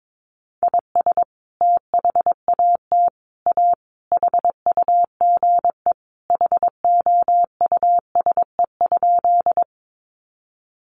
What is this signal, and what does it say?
Morse code, heard as a keyed tone, about 22 wpm, 700 Hz. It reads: IS THAT A HUGE HOUSE?
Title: Sentences from Top 1100 Words 22wpm